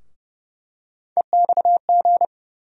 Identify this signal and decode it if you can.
Morse code, heard as a keyed tone, about 30 wpm, 700 Hz. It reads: EXG
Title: USA Call Sign Postfixes - 3 Letters 30wpm